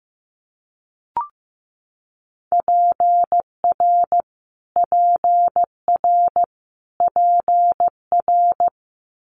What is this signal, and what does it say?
Morse code, heard as a keyed tone, about 15 wpm, 700 Hz. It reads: PR PR PR
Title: Multi-letter Phonemes Repeated 3x 15wpm